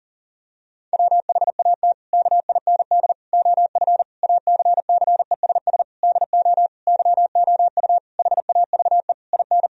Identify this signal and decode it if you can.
Morse code, heard as a keyed tone, about 40 wpm, 700 Hz. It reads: WHAT KIND OF ACCESS DO YOU HAVE IN
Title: Sentences from Top 1000 Words - Review Part 4 - 40wpm